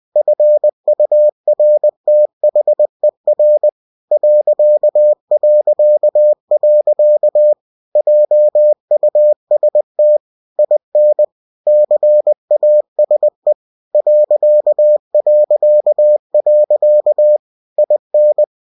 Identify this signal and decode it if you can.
Morse code, heard as a keyed tone, about 20 wpm, 600 Hz. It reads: FURTHER ... JUST IN CASE ... IN
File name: cw-books / winnie-the-pooh / winnie-20wpm-page-0037